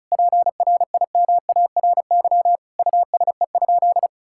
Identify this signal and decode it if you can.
Morse code, heard as a keyed tone, about 34 wpm, 700 Hz. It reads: PRIMARY USE?
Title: Sentences from Top 1000 Words - Review Part 5 - 35wpm